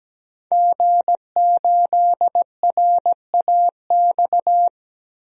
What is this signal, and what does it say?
Morse code, heard as a keyed tone, about 17 wpm, 700 Hz. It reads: G8RAX
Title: DXCC-100 Call Signs - 5 Characters - 17wpm